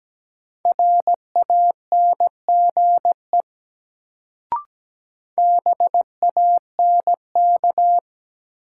Morse code, heard as a keyed tone, about 17 wpm, 700 Hz. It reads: RANGE BANK
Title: Sentences from Top 900 Words - New Word Drill - 17wpm